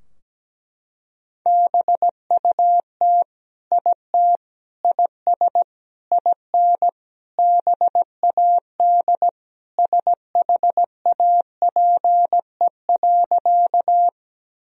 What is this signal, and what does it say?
Morse code, heard as a keyed tone, about 17 wpm, 700 Hz. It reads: BUT IT IS IN BAD SHAPE.
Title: Sentences from Top 1000 Words - Review Part 3 - 17wpm